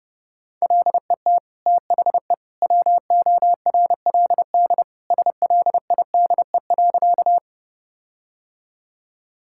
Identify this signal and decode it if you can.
Morse code, heard as a keyed tone, about 32 wpm, 700 Hz. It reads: LET THE WORLD SLIDE.
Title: Sentences from Top 1600 Words 30wpm